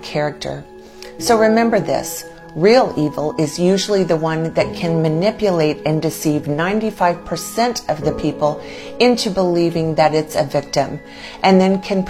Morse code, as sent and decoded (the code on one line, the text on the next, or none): none